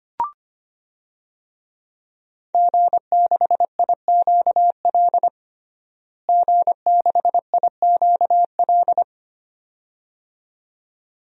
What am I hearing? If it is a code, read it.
Morse code, heard as a keyed tone, about 25 wpm, 700 Hz. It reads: G6IQL G6IQL
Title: DXCC-100 Call Signs - 5 Characters - Repeated 2x - 25wpm